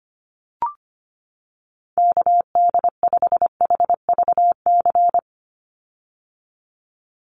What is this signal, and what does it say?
Morse code, heard as a keyed tone, about 25 wpm, 700 Hz. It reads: KD5HVC
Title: USA Call Signs 25wpm